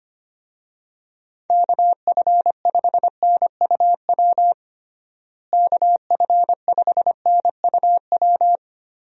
Morse code, heard as a keyed tone, about 25 wpm, 700 Hz. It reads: KF5NUW KF5NUW
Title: USA Call Signs - 6 Characters - Repeated 2x - 25wpm